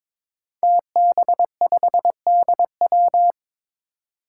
Morse code, heard as a keyed tone, about 22 wpm, 700 Hz. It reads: TB5DW
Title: Five Challenging Letter-Number Combinations 22wpm